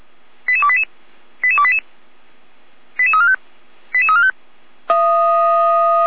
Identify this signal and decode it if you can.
Five-tone selective call (ZVEI), heard as quick runs of five tones, tones of about 70 ms. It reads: 88288 88288 88345 88345
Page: SDR/ZVEI